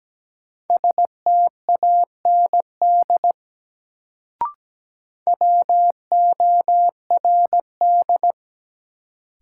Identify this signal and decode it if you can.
Morse code, heard as a keyed tone, about 17 wpm, 700 Hz. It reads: STAND WORD
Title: Sentences from Top 300 Words - New Word Drill - 17wpm